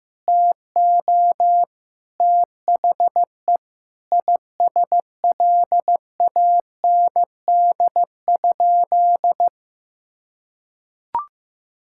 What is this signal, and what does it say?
Morse code, heard as a keyed tone, about 15 wpm, 700 Hz. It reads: TO THE ISLAND?
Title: Sentences from Top 1600 Words 15wpm